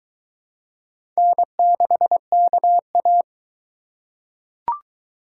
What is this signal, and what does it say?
Morse code, heard as a keyed tone, about 22 wpm, 700 Hz. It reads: N6KA